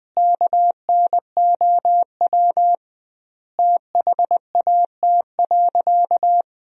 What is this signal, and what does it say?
Morse code, heard as a keyed tone, about 20 wpm, 700 Hz. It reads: KNOW THAT.
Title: 2x Word-Spacing - Sentences from Top 500 Words - 20wpm